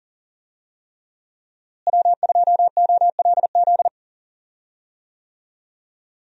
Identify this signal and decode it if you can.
Morse code, heard as a keyed tone, about 37 wpm, 700 Hz. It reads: W2OLZ
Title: USA Call Signs - 5 Characters - Repeated 3x - 40wpm